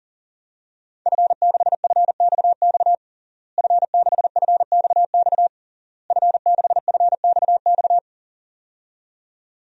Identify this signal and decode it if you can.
Morse code, heard as a keyed tone, about 37 wpm, 700 Hz. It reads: F6FXX F6FXX F6FXX
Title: DXCC-100 Call Signs - 5 Characters - Repeated 3x - 40wpm